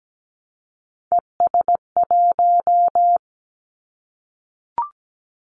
Morse code, heard as a keyed tone, about 17 wpm, 700 Hz. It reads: ES1
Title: Callsign Prefixes for Top 100 DXCC - 17wpm